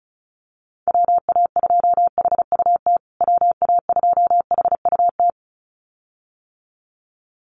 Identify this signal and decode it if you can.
Morse code, heard as a keyed tone, about 34 wpm, 700 Hz. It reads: WA2HUT WA2HUT
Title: USA Call Signs Repeated 2x 35wpm